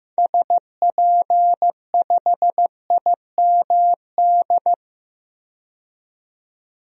Morse code, heard as a keyed tone, about 15 wpm, 700 Hz. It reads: SP5IMD
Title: DXCC-100 Call Signs - 6 Characters - 15wpm